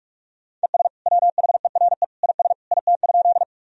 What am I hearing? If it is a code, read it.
Morse code, heard as a keyed tone, about 41 wpm, 700 Hz. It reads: ES WHERE IS IT?
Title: Sentences from Top 200 Words and Abbreviations 45wpm